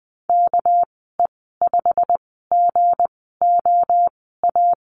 Morse code, heard as a keyed tone, about 20 wpm, spaced as for 15 wpm, 700 Hz. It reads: KE5GOA
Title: 207 Lesson 33g - Letter j - Call Signs 20wpm